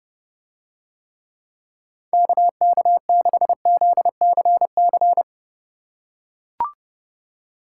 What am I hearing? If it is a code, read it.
Morse code, heard as a keyed tone, about 30 wpm, 700 Hz. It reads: KK6ZCC